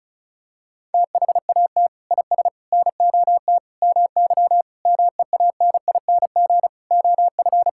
Morse code, heard as a keyed tone, about 36 wpm, 700 Hz. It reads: THAT IS NOT MY MEANING OF
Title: Sentences from Top 1000 Words - Review Part 6 - 35wpm